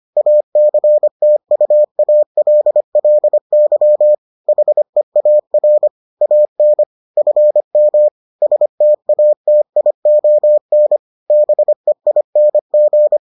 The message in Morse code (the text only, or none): ACTUALLY HEAR AN FM STATION BEING